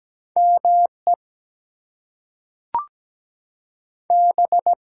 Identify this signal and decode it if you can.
Morse code, heard as a keyed tone, about 17 wpm, 700 Hz. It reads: ME B
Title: Two Letter Words - 17wpm